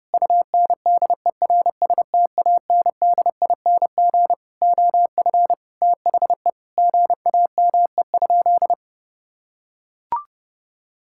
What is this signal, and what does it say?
Morse code, heard as a keyed tone, about 30 wpm, 700 Hz. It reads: UNDERSTANDING OF THE GAME?
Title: Sentences from Top 1000 Words - Review Part 5 - 30wpm